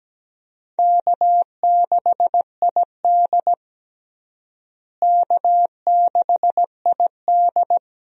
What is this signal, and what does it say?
Morse code, heard as a keyed tone, about 17 wpm, 700 Hz. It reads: K6ID K6ID
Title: USA Call Signs - 4 Characters - Repeated 2x - 17wpm